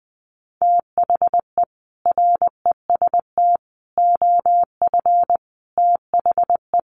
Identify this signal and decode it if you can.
Morse code, heard as a keyed tone, about 20 wpm, 700 Hz. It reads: THE REST OF THE
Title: Sets of 5 Words 20wpm